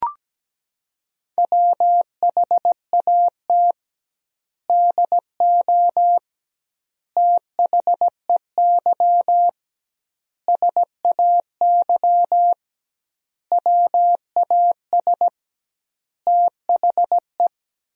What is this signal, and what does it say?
Morse code, heard as a keyed tone, about 17 wpm, 700 Hz. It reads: WHAT DO THEY SAY WAS THE
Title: 2x Word-Spacing - Sentences from Top 200 Words - 17wpm